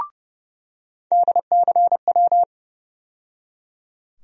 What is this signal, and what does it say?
Morse code, heard as a keyed tone, about 30 wpm, 700 Hz. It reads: DCW
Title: USA Call Sign Postfixes - 3 Letters 30wpm